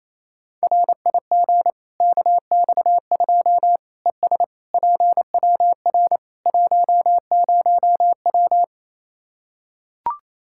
Morse code, heard as a keyed tone, about 27 wpm, 700 Hz. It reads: RIG KX2 ES PWR 10W
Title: K1USN SST QSO Practice 28wpm